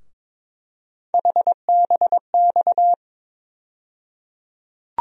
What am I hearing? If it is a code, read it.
Morse code, heard as a keyed tone, about 22 wpm, 700 Hz. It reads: HBX
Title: USA Call Sign Postfixes - 3 Letters 22wpm